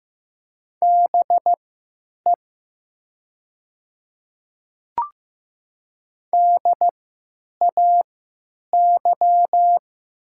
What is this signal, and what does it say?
Morse code, heard as a keyed tone, about 15 wpm, spaced as for 9 wpm, 700 Hz. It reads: BE DAY
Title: 3x Character Spacing - Top 100 Words - 15wpm